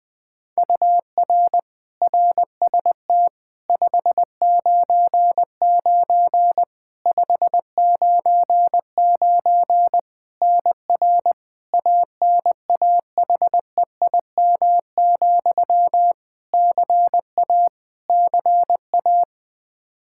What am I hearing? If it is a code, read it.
Morse code, heard as a keyed tone, about 21 wpm, 700 Hz. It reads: UR RST 599 599 NR ANAHEIM, CA CA
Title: Practice QSO - Exchange 1 - Beginning - 20wpm